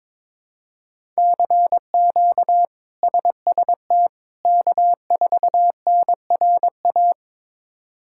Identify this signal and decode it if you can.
Morse code, heard as a keyed tone, about 22 wpm, 700 Hz. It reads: CQ SST K4NRA
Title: K1USN SST QSO Practice - 22wpm